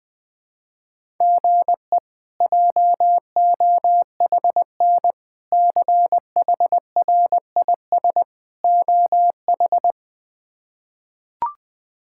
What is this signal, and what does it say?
Morse code, heard as a keyed tone, about 20 wpm, 700 Hz. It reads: GE JOHN CHRIS OH
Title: K1USN SST QSO Practice - 20wpm